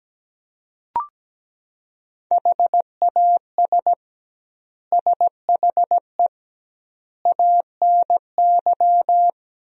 Morse code, heard as a keyed tone, about 17 wpm, 700 Hz. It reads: HAS SHE ANY
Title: 2x Word-Spacing - Sentences from Top 400 Words - 17wpm